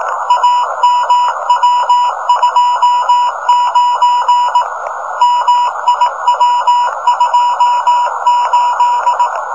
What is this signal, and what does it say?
Morse code, heard as a keyed tone, about 18 wpm, 1000 Hz, with noise. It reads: AMW29 MIW28
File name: rs-15 cw telemetry